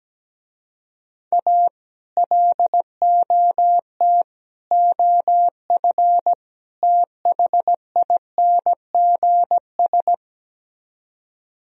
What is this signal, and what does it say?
Morse code, heard as a keyed tone, about 17 wpm, 700 Hz. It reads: A LOT OF THINGS
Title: Speed-Racing - Sets of 4 Words 17wpm